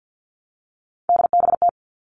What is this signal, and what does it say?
Morse code, heard as a keyed tone, about 48 wpm, 700 Hz. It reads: DBT
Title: All Cut Numbers - 3-Digit 50wpm